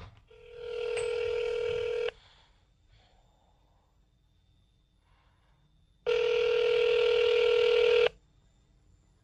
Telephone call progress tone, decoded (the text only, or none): ringback tone